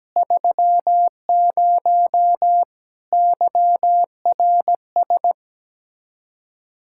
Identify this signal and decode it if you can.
Morse code, heard as a keyed tone, about 17 wpm, 700 Hz. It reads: 30 YRS
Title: Practice QSO - Exchange 3 - More Info - 17wpm